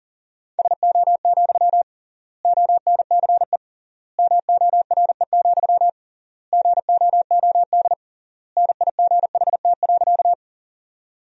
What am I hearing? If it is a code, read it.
Morse code, heard as a keyed tone, about 40 wpm, 700 Hz. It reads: SO, ONCE MORE, GOOD NIGHT.
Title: 3x Word-Spacing - Sentences from Top 300 Words - 40wpm